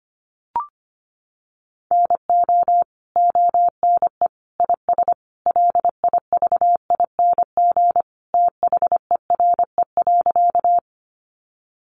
Morse code, heard as a keyed tone, about 25 wpm, 700 Hz. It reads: NO ONE IS LIVING THERE.
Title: Sentences from Top 1100 Words 25wpm